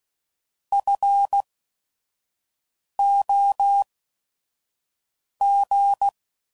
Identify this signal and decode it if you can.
Morse code, heard as a keyed tone, about 16 wpm, spaced as for 5 wpm, 800 Hz. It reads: FOG
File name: Prefixes / By First Letter / Fs